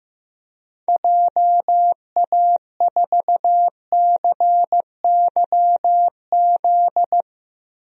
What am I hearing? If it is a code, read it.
Morse code, heard as a keyed tone, about 16 wpm, 700 Hz. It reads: JA4CYZ
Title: DXCC-100 Call Signs - 6 Characters - Repeated 3x - 15wpm